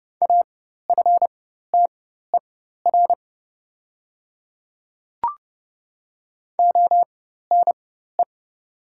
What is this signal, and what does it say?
Morse code, heard as a keyed tone, about 30 wpm, spaced as for 14 wpm, 700 Hz. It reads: AFTER ONE
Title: 4x Character Spacing - Top 200 Words - 30wpm